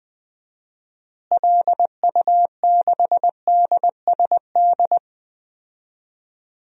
Morse code, heard as a keyed tone, about 20 wpm, 700 Hz. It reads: LU6DSD
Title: DXCC-100 Call Signs - 6 Characters - Repeated 3x - 20wpm